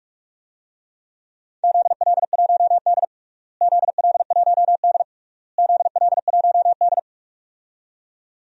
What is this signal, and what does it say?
Morse code, heard as a keyed tone, about 46 wpm, 700 Hz. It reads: ZL1D ZL1D ZL1D
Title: DXCC-100 Call Signs - 4 Characters - Repeated 3x - 45wpm